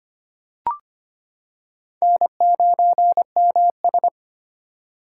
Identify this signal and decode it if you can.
Morse code, heard as a keyed tone, about 25 wpm, 700 Hz. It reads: N9MS
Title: K1USN SST QSO Practice - 25wpm